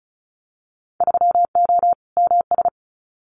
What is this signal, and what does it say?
Morse code, heard as a keyed tone, about 34 wpm, 700 Hz. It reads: 3O MS
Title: Winter Field Day Exchanges Encore 35wpm